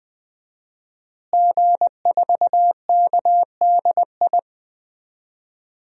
Morse code, heard as a keyed tone, about 20 wpm, 700 Hz. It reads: G4KDI
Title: DXCC-100 Call Signs - 5 Characters - Repeated 3x - 20wpm